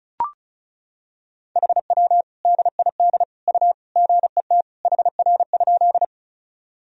Morse code, heard as a keyed tone, about 34 wpm, 690 Hz. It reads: HW DID U GET HR?
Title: Sentences from Top 200 Words and Abbreviations 35wpm